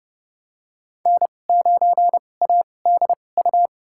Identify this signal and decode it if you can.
Morse code, heard as a keyed tone, about 32 wpm, spaced as for 22 wpm, 700 Hz. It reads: N9ADU
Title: USA Call Signs - 5 Characters - 2x Character Spacing - 30wpm